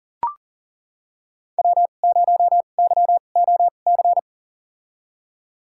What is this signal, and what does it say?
Morse code, heard as a keyed tone, about 40 wpm, spaced as for 29 wpm, 700 Hz. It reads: W0YOC
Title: USA Call Signs - 5 Characters - 2x Character Spacing - 40wpm